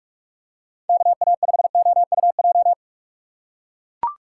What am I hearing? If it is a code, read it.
Morse code, heard as a keyed tone, about 43 wpm, 700 Hz. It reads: KA5OUJ
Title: USA Call Signs - 6 Characters - 4x Character Spacing - 45wpm